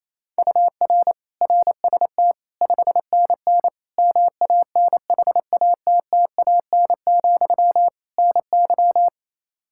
Morse code, heard as a keyed tone, about 28 wpm, 700 Hz. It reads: UR RST 5NN MANHATTAN, NY
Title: Practice QSO - Exchange 1 - Beginning - 28wpm